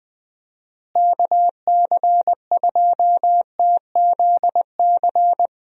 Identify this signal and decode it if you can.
Morse code, heard as a keyed tone, about 20 wpm, 700 Hz. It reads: KC2TZC
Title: USA Call Signs - 6 Characters - Repeated 2x - 20wpm